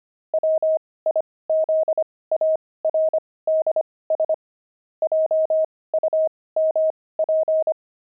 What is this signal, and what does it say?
Morse code, heard as a keyed tone, about 25 wpm, spaced as for 18 wpm, 630 Hz. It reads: WIZARDS JUMP
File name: pangram-fs-18-cs-25-wizards